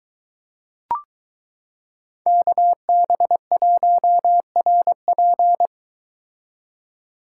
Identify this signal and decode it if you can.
Morse code, heard as a keyed tone, about 22 wpm, 700 Hz. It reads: KB1RP